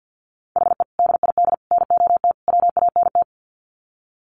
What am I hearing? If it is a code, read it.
Morse code, heard as a keyed tone, about 48 wpm, 700 Hz. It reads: HE DID NOT WANT
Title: Sets of 4 Words 50wpm